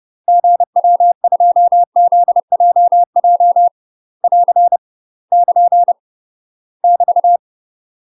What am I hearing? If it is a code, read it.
Morse code, heard as a keyed tone, about 30 wpm, 700 Hz. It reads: GW2ZJJ <AR> <KN> <BT>
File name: UK MASTER QSOs TRAINING / 30WPM / UK MASTER QSOs 0000